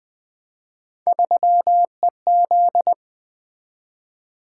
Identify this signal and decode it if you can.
Morse code, heard as a keyed tone, about 20 wpm, 700 Hz. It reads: 3EZ
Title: Three Letter-Number Combinations - 20wpm